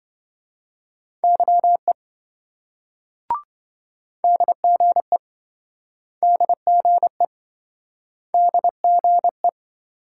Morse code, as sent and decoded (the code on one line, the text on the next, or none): -.-- . / -.. --. / -.. --. / -.. --. .
YE DGE DGE DGE